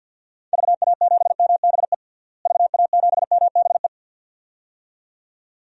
Morse code, heard as a keyed tone, about 47 wpm, 700 Hz. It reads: VA7MBE VA7MBE
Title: DXCC-100 Call Signs - 6 Characters - Repeated 2x - 50wpm